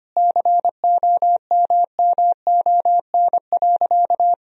Morse code, heard as a keyed tone, about 25 wpm, 700 Hz. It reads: COMMON.
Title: Sentences from Top 1000 Words - Review Part 3 - 25wpm